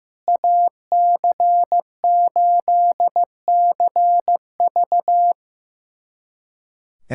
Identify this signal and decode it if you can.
Morse code, heard as a keyed tone, about 16 wpm, 700 Hz. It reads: AC8CV
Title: USA Callsign DE Callsign - 5 Characters - 15wpm